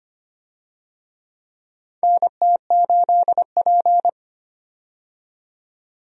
Morse code, heard as a keyed tone, about 25 wpm, 700 Hz. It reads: NT8P
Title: USA Call Signs - 4 Characters - 25wpm